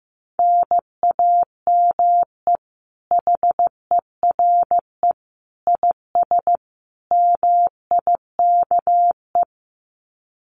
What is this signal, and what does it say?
Morse code, heard as a keyed tone, about 15 wpm, 700 Hz. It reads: NAME HERE IS MIKE
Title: Common QSO Phrases 15wpm